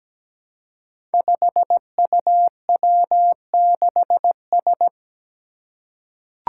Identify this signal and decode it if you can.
Morse code, heard as a keyed tone, about 17 wpm, 700 Hz. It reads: 5UW6S
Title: Five Challenging Letter-Number Combinations 17wpm